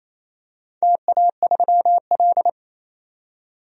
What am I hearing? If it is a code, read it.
Morse code, heard as a keyed tone, about 28 wpm, 700 Hz. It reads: TA3L
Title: DXCC-100 Call Signs - 4 Characters - 28wpm